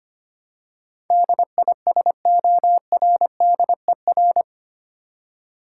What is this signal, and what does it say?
Morse code, heard as a keyed tone, about 25 wpm, 700 Hz. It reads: DISORDER
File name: Eight Letter Words - 25wpm